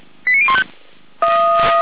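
Five-tone selective call (ZVEI), heard as quick runs of five tones, tones of about 70 ms. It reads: 89015
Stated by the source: ZVEI five-tone selective call received by SDR.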